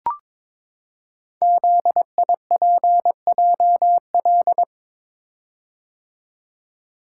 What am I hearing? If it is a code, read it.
Morse code, heard as a keyed tone, about 22 wpm, 700 Hz. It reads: ZIPJL
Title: Code Groups - 5 Letter:Numbers - 22wpm